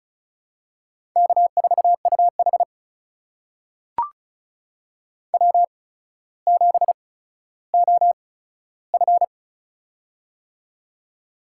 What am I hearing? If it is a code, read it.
Morse code, heard as a keyed tone, about 36 wpm, spaced as for 15 wpm, 700 Hz. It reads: K4UH W7OF
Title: USA Call Signs - 4 Characters - 8x Character Spacing - 35wpm